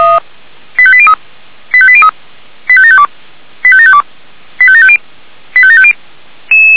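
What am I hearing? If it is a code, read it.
Five-tone selective call (ZVEI), heard as quick runs of five tones, tones of about 70 ms. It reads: 85803 85803 85752 85752 85750 85750
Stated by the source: ZVEI five-tone selective call received on SDR, from feuerwehr-orlamuende.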